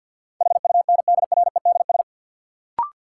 Morse code, heard as a keyed tone, about 47 wpm, 700 Hz. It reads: HUNDREDS
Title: Sets of 4 Words - New Word Drill - 50wpm